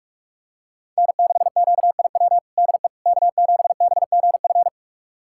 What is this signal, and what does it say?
Morse code, heard as a keyed tone, about 44 wpm, 700 Hz. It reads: N6QIW DE K7BGF
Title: USA Callsign DE Callsign - 5 Characters - 45wpm